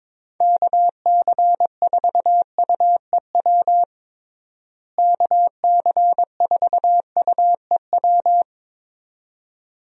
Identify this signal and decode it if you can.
Morse code, heard as a keyed tone, about 22 wpm, 700 Hz. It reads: KC4UEW KC4UEW
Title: USA Call Signs - 6 Characters - Repeated 2x - 22wpm